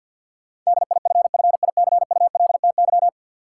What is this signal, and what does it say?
Morse code, heard as a keyed tone, about 51 wpm, 700 Hz. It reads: DIFFICULTY
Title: Sentences from Top 1500 Words - New Word Drill - 50wpm